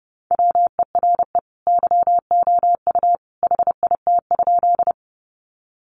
Morse code, heard as a keyed tone, about 30 wpm, 700 Hz. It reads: WERE YOU HIT?
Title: Sentences from Top 1200 Words 30wpm